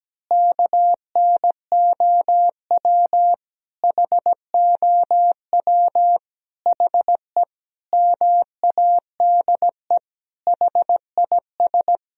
Morse code, heard as a keyed tone, about 17 wpm, 700 Hz. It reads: KNOW HOW HE MADE HIS
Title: Sentences from Top 1000 Words - Review Part 1 - 17wpm